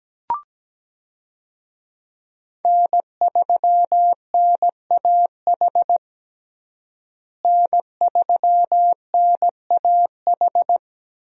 Morse code, heard as a keyed tone, about 17 wpm, 700 Hz. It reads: N3NAH N3NAH
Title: USA Call Signs - 5 Characters - Repeated 2x - 17wpm